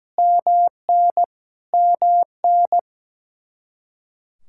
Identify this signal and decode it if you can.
Morse code, heard as a keyed tone, about 17 wpm, 700 Hz. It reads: MN MN
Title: Practice QSO - Exchange 1 - Beginning - 17wpm